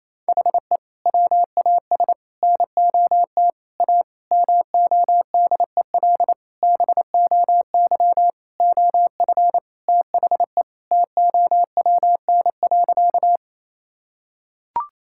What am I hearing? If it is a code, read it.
Morse code, heard as a keyed tone, about 28 wpm, 700 Hz. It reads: HE WAS NOT A MODEL BOY OF THE TOWN.
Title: Sentences from Top 1000 Words - Review Part 5 - 28wpm